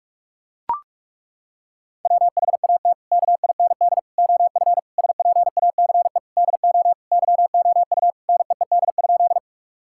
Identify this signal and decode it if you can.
Morse code, heard as a keyed tone, about 42 wpm, 700 Hz. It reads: WHAT KIND OF SPACE DO YOU NEED?
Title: Sentences from Top 1000 Words - Review Part 5 - 45wpm